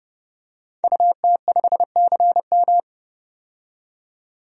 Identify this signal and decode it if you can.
Morse code, heard as a keyed tone, about 30 wpm, 700 Hz. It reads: UT5CM
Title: DXCC-100 Call Signs - 5 Characters - 30wpm